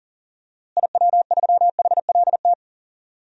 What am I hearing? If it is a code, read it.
Morse code, heard as a keyed tone, about 40 wpm, 700 Hz. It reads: IW3HLT
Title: DXCC-100 Call Signs - 6 Characters - Repeated 3x - 40wpm